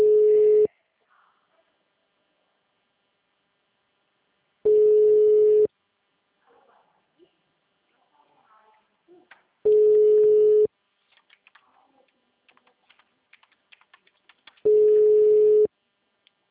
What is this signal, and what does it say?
Ringback tone (Europe, 425 Hz).